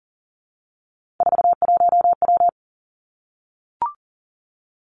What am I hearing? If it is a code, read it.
Morse code, heard as a keyed tone, about 38 wpm, 700 Hz. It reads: 41W
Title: Three Challenging Letter-Number Combinations 40wpm